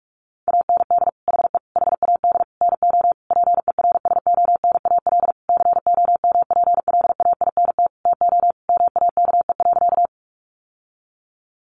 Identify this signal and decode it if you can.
Morse code, heard as a keyed tone, about 44 wpm, 700 Hz. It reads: AND HE HAD NO PERSONAL COMPLAINT TO MAKE.
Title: Sentences from Top 1400 Words 45wpm